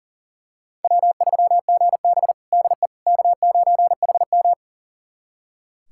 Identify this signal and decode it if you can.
Morse code, heard as a keyed tone, about 37 wpm, 700 Hz. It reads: W3GB DE K9HM
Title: USA Callsign DE Callsign - 4 Characters - 40wpm